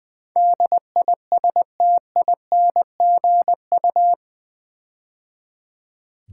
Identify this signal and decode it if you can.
Morse code, heard as a keyed tone, about 20 wpm, 700 Hz. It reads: DISTINGU
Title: Build A Word - 4 Letters At A Time - 11-Letter Words - 20wpm